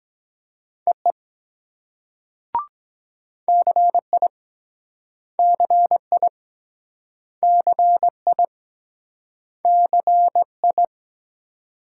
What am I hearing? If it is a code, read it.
Morse code, heard as a keyed tone, about 23 wpm, 700 Hz. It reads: EE CI CI CI CI